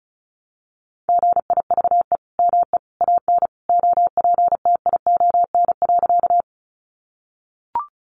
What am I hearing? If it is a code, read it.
Morse code, heard as a keyed tone, about 34 wpm, 700 Hz. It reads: GIVE ME AN OPTION.
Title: Sentences from Top 1300 Words 35wpm